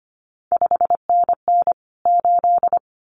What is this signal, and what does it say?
Morse code, heard as a keyed tone, about 25 wpm, 700 Hz. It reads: 5NN 8
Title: CQ WW DX Contest Exchange - 25wpm